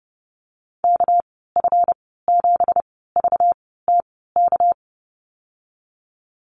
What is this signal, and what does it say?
Morse code, heard as a keyed tone, about 32 wpm, spaced as for 17 wpm, 700 Hz. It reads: KF7VTK
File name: Lesson 37g - Letter x - Call Signs 30wpm